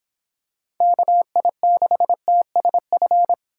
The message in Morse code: KI6TSF

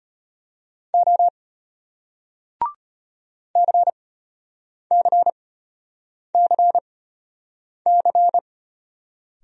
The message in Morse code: O CCCC

